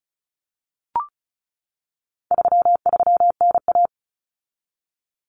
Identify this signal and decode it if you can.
Morse code, heard as a keyed tone, about 34 wpm, 700 Hz. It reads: 33NA
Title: Common Cut Numbers - 4-Digit 35wpm